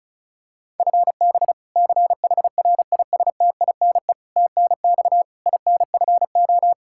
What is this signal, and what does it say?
Morse code, heard as a keyed tone, about 36 wpm, 700 Hz. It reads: FB CHRISTINE TNX INFO
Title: Practice QSO - Exchange 3 - More Info - 35wpm